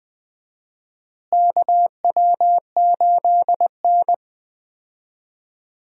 Morse code, heard as a keyed tone, about 20 wpm, 700 Hz. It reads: KW8N
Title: USA Call Signs - 4 Characters - Repeated 2x - 20wpm